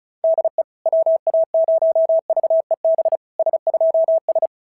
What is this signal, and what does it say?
Morse code, heard as a keyed tone, about 36 wpm, 690 Hz. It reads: DE WA0VEB S2S
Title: SOTA QSO Practice 35wpm